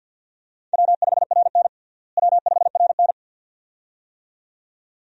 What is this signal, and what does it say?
Morse code, heard as a keyed tone, about 51 wpm, 700 Hz. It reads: W5RN W5RN
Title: USA Call Signs - 4 Characters - Repeated 2x - 50wpm